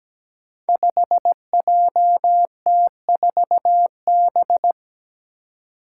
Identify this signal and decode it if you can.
Morse code, heard as a keyed tone, about 17 wpm, 700 Hz. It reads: 5JT4B
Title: Five Challenging Letter-Number Combinations 17wpm